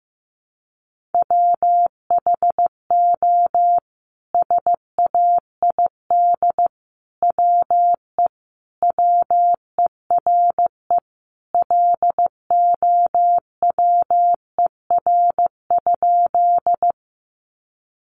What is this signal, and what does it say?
Morse code, heard as a keyed tone, about 15 wpm, 700 Hz. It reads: WHO SAID WE WERE LOWER?
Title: Sentences from Top 700 Words 15wpm